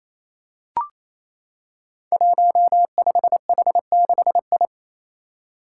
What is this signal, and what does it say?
Morse code, heard as a keyed tone, about 29 wpm, 700 Hz. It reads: 15H6I
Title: Five Challenging Letter-Number Combinations 28wpm